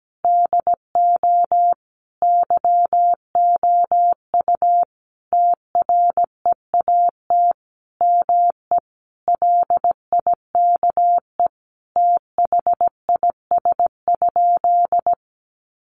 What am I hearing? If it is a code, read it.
Morse code, heard as a keyed tone, about 17 wpm, 700 Hz. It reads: DO YOU TREAT ME LIKE THIS?
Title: Sentences from Top 1300 Words 17wpm